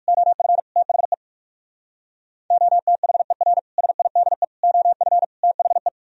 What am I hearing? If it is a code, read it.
Morse code, heard as a keyed tone, about 63 wpm, spaced as for 54 wpm, 700 Hz. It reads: OF THE OTHER SIDE OF THE
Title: Speed-Racing - Sets of 4 Words 45wpm